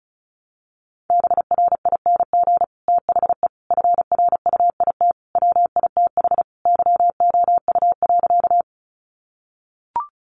Morse code, heard as a keyed tone, about 34 wpm, 700 Hz. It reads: BRING THE FRUIT WITH YOU.
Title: Sentences from Top 1300 Words 35wpm